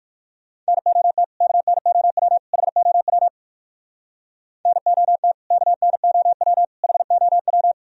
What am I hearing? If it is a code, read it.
Morse code, heard as a keyed tone, about 67 wpm, spaced as for 57 wpm, 700 Hz. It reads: NOT KNOW HOW NOT KNOW HOW